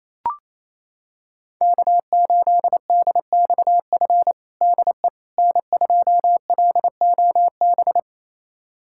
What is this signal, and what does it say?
Morse code, heard as a keyed tone, about 28 wpm, 700 Hz. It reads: K8DXF DE N2LOB
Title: USA Callsign DE Callsign - 5 Characters - 28wpm